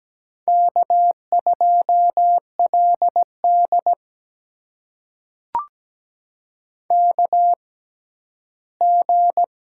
Morse code, heard as a keyed tone, about 17 wpm, spaced as for 11 wpm, 700 Hz. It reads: K2LD KG
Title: USA Call Signs - 6x Character Spacing - 17wpm